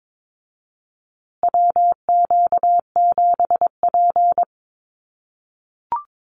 Morse code, heard as a keyed tone, about 22 wpm, 700 Hz. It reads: WQ7P
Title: USA Call Signs Repeated 2x 22wpm